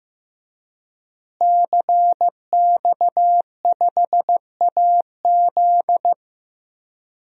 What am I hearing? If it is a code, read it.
Morse code, heard as a keyed tone, about 15 wpm, 700 Hz. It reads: CX5AZ
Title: DXCC-100 Call Signs - 5 Characters - Repeated 3x - 15wpm